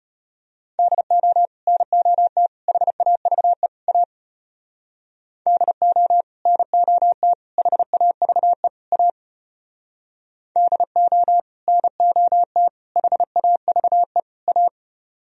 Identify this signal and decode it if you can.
Morse code, heard as a keyed tone, about 30 wpm, 700 Hz. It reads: DO NOT HAVE A DO NOT HAVE A DO NOT HAVE A
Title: Speed-Racing - Sets of 4 Words 25wpm